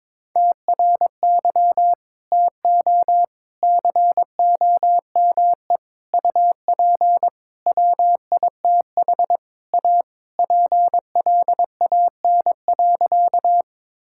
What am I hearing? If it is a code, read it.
Morse code, heard as a keyed tone, about 22 wpm, 700 Hz. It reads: TRY TO COME UP WITH A PLAN.
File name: practice / 22wpm / Sentences with 4-Word Phrases - 22wpm